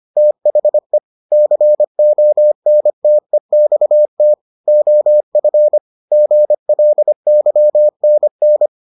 Morse code, heard as a keyed tone, about 25 wpm, 600 Hz. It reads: THE CONTEXT OF GLYNN